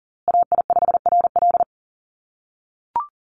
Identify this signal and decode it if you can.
Morse code, heard as a keyed tone, about 40 wpm, 700 Hz. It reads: AI5RL
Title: USA Call Signs Repeated 2x 40wpm